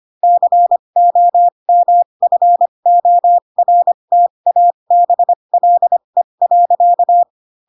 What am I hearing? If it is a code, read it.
Morse code, heard as a keyed tone, about 25 wpm, spaced as for 22 wpm, 700 Hz. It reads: COMFORTABLE.
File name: Pinocchio / Pinocchio ch3 18 25 ik1hge